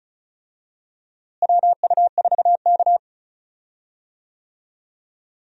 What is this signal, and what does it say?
Morse code, heard as a keyed tone, about 36 wpm, 700 Hz. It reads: WU4K
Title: USA Call Signs - 4 Characters - Repeated 3x - 35wpm